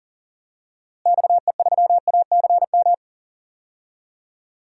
Morse code, heard as a keyed tone, about 42 wpm, 700 Hz. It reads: XE3ACM
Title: DXCC-100 Call Signs - 6 Characters - Repeated 3x - 40wpm